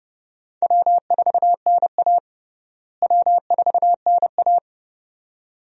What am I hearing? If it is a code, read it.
Morse code, heard as a keyed tone, about 30 wpm, 700 Hz. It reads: W4NA W4NA
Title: USA Call Signs - 4 Characters - Repeated 2x - 30wpm